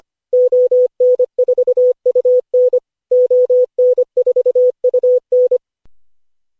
Morse code, heard as a keyed tone, about 25 wpm, 500 Hz. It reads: ON4UN ON4UN
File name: cw / Session5 cw training